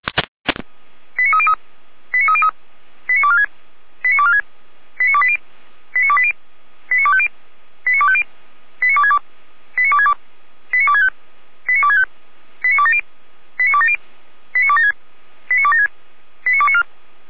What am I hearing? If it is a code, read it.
Five-tone selective call (ZVEI), heard as quick runs of five tones, tones of about 70 ms. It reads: 89303 89303 89246 89246 89280 89280 89250 89250 89262 89262 89265 89265 89270 89270 89276 89276 89294